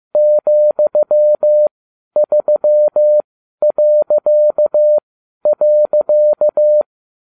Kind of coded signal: Morse code